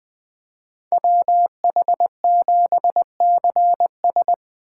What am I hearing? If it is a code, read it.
Morse code, heard as a keyed tone, about 20 wpm, 700 Hz. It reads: WH7CS
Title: Speed Racing - USA 2x2 Call Signs 17wpm